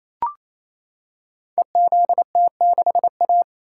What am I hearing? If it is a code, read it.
Morse code, heard as a keyed tone, about 28 wpm, 690 Hz. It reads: EZT6A
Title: Code Groups - 5 Letter:Numbers - 28wpm